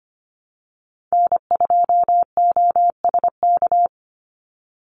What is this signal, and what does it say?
Morse code, heard as a keyed tone, about 25 wpm, 700 Hz. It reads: N2OSK
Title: USA Call Signs 25wpm